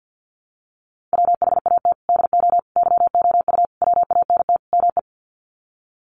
Morse code, heard as a keyed tone, about 50 wpm, 700 Hz. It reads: WHAT DO YOU WANT ME
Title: Sets of 5 Words Encore2 50wpm